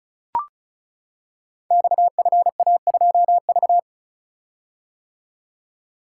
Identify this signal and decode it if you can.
Morse code, heard as a keyed tone, about 36 wpm, 700 Hz. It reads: XFA2V
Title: Code Groups - 5 Letter:Numbers - 35wpm